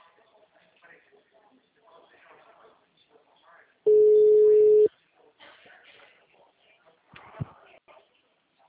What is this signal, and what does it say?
Ringback tone (Europe, 425 Hz), with speech.